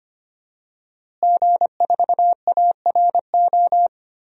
Morse code, heard as a keyed tone, about 25 wpm, 700 Hz. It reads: G4ARO
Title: DXCC-100 Call Signs - 5 Characters - Repeated 2x - 25wpm